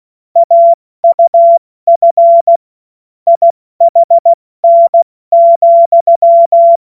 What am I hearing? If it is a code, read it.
Morse code, heard as a keyed tone, about 16 wpm, 680 Hz. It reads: AUF IHN,
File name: Morsekurs / Witze / Witz 28. Dezember 2021, 70 F80